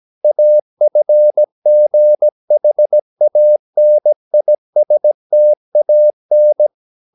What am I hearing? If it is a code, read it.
Morse code, heard as a keyed tone, about 17 wpm, 600 Hz. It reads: AFGHANISTAN
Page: Romanen Never af Ken Follet på engelsk.